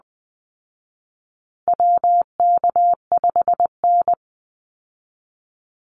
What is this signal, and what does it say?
Morse code, heard as a keyed tone, about 20 wpm, 700 Hz. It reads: WK5N
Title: Call Signs Repeated 2x - 20wpm